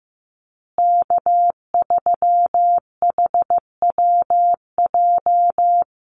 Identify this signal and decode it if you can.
Morse code, heard as a keyed tone, about 15 wpm, 700 Hz. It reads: K3HWJ
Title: USA Call Signs Repeated 3x 15wpm